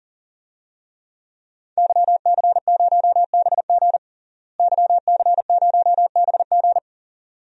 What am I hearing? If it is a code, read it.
Morse code, heard as a keyed tone, about 40 wpm, 700 Hz. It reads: YC0BG YC0BG
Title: DXCC-100 Call Signs - 5 Characters - Repeated 2x - 40wpm